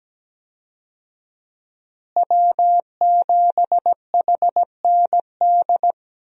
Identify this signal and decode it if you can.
Morse code, heard as a keyed tone, about 17 wpm, 700 Hz. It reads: W7HND
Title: USA Call Signs - 5 Characters - Repeated 2x - 17wpm